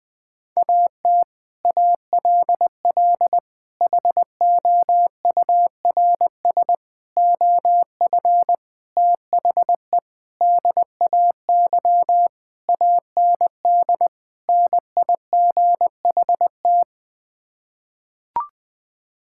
Morse code, heard as a keyed tone, about 21 wpm, 700 Hz. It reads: AT ALL HOURS OF THE DAY AND NIGHT
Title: Sets of 8 Words - 20wpm